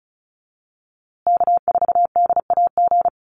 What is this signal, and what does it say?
Morse code, heard as a keyed tone, about 34 wpm, 700 Hz. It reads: K4DAG
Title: USA Call Signs Repeated 3x 35wpm